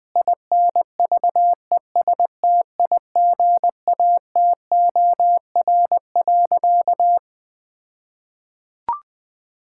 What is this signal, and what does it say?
Morse code, heard as a keyed tone, about 20 wpm, 700 Hz. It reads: INVESTIGATOR.
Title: All-Age Appropriate Jokes - 20wpm